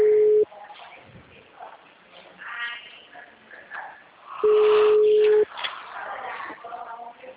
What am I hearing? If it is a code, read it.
Ringback tone (Europe, 425 Hz), with speech.
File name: out- -2003- - - .81359